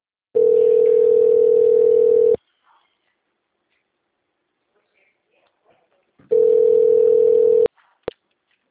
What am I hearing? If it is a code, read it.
Ringback tone (North America), 440 + 480 Hz.
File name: out- -2015- - - .78065